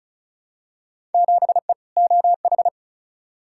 Morse code, heard as a keyed tone, about 36 wpm, 700 Hz. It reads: 7E OH
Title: Field Day Exchanges Encore - 35wpm